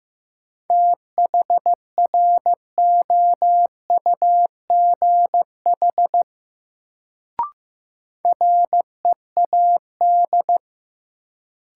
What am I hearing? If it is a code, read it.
Morse code, heard as a keyed tone, about 15 wpm, 700 Hz. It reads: THROUGH READ
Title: Sentences from Top 200 Words - New Word Drill - 15wpm